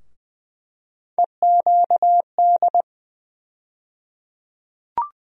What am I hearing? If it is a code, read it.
Morse code, heard as a keyed tone, about 20 wpm, 700 Hz. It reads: EQD
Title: USA Call Sign Postfixes - 3 Letters 20wpm